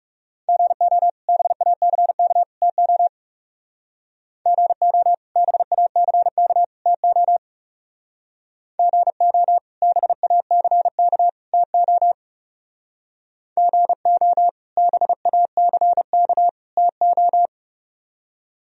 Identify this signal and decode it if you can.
Morse code, heard as a keyed tone, about 36 wpm, 700 Hz. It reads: GO BACK TO GO BACK TO GO BACK TO GO BACK TO
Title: Speed-Racing - Sets of 3 Words 30wpm